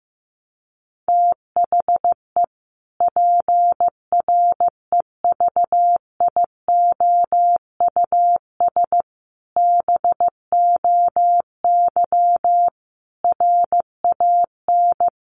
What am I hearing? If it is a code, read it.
Morse code, heard as a keyed tone, about 14 wpm, 700 Hz. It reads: THE PREVIOUS BOY RAN
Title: Sentences from Top 1500 Words 15wpm